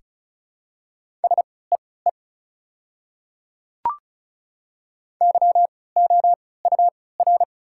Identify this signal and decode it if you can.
Morse code, heard as a keyed tone, about 36 wpm, spaced as for 20 wpm, 690 Hz. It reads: SEE YOUR
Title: 3x Character Spacing - Top 100 Words - 35wpm